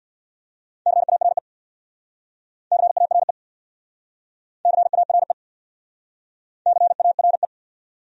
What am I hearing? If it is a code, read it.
Morse code, heard as a keyed tone, about 64 wpm, 700 Hz. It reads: CARE CARE CARE CARE